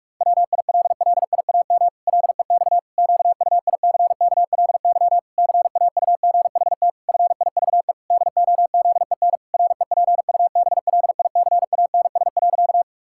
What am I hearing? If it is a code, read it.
Morse code, heard as a keyed tone, about 41 wpm, 700 Hz. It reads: WILLIAM LEX QUICKLY CAUGHT FIVE DOZEN REPUBLICANS.
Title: Pangram Challenge v2 - 45wpm